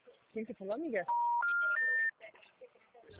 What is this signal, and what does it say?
Special information tone, with speech.
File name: out- -2004- - - .15